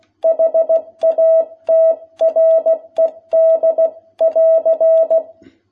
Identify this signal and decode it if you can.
Morse code, heard as a keyed tone, about 16 wpm, 660 Hz. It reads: HATRED <AR>
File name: jh8cbh / 2021cw